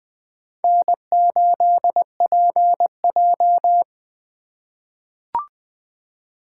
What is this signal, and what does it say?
Morse code, heard as a keyed tone, about 20 wpm, 700 Hz. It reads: N8PJ
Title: USA Call Signs - 3x Character Spacing - 20wpm